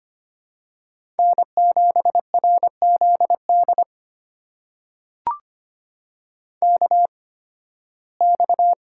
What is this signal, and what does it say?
Morse code, heard as a keyed tone, about 25 wpm, spaced as for 16 wpm, 700 Hz. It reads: N7RZD KX
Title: USA Call Signs - 5 Characters - 8x Character Spacing - 25wpm